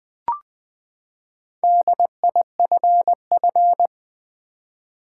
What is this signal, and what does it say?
Morse code, heard as a keyed tone, about 20 wpm, 700 Hz. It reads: DIFF
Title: Build A Word - 4 Letters At A Time - 10-Letter Words - 20wpm